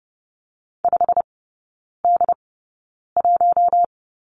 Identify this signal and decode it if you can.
Morse code, heard as a keyed tone, about 30 wpm, spaced as for 9 wpm, 700 Hz. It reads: 5D1
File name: Lesson 16e - Letter c - Three Characters 30wpm